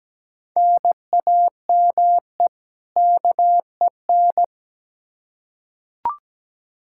Morse code, heard as a keyed tone, about 17 wpm, 700 Hz. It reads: NAME KEN
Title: K1USN SST QSO Practice 17wpm